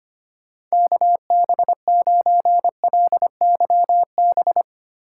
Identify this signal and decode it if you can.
Morse code, heard as a keyed tone, about 25 wpm, 700 Hz. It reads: KB9LYB
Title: USA Call Signs - 6 Characters - 2x Character Spacing - 25wpm